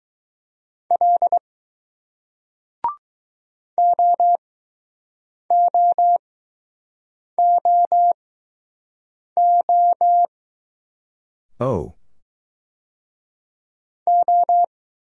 Morse code, heard as a keyed tone, about 20 wpm, spaced as for 6 wpm, 700 Hz. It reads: L OOOO O